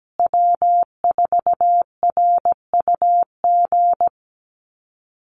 Morse code, heard as a keyed tone, about 17 wpm, 700 Hz. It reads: W4RUG
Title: USA Call Signs Repeated 2x 17wpm